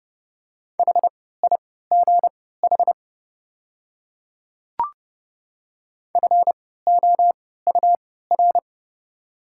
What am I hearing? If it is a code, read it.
Morse code, heard as a keyed tone, about 30 wpm, spaced as for 17 wpm, 690 Hz. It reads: HIGH FOUR
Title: 3x Character Spacing - Top 500 Words - 30wpm